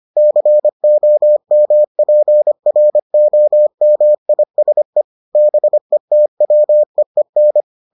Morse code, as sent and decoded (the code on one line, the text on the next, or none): -.-. --- -- .--. .-. --- -- .. ... / -... - .-- . -.
COMPROMISE BETWEEN